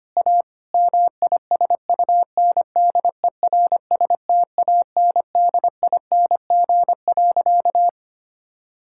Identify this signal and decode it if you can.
Morse code, heard as a keyed tone, about 25 wpm, 700 Hz. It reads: A MISUNDERSTANDING.
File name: practice / 25wpm / Sentences with 4-Word Phrases - 25wpm